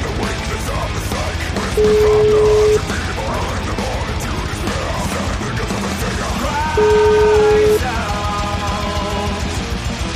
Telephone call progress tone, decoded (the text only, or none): ringback tone